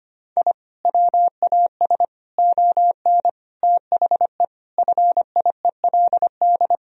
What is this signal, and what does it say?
Morse code, heard as a keyed tone, about 25 wpm, 700 Hz. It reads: I WAS ON THE FIELD
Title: Sentences from Top 1000 Words - Review Part 2 - 25wpm